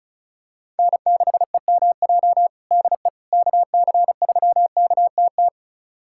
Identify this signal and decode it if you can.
Morse code, heard as a keyed tone, about 36 wpm, 700 Hz. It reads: N6EMJ DE KC3KTT
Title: USA Callsign DE Callsign 35wpm